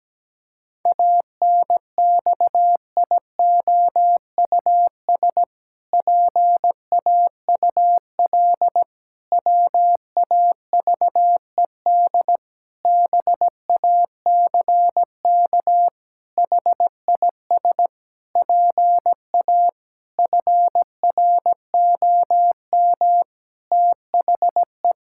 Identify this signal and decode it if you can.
Morse code, heard as a keyed tone, about 17 wpm, 700 Hz. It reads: ANXIOUS PAUL WAVED BACK HIS PA FROM THE
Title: Pangram Challenge v2 - 17wpm